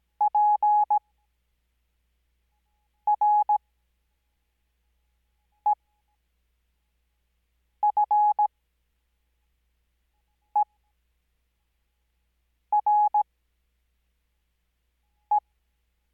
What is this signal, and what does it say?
Morse code, heard as a keyed tone, about 17 wpm, spaced as for 4 wpm, 840 Hz. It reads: PREFERE
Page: Farnsworth Spacing: 15WPM Character speed Spaced at 5WPM.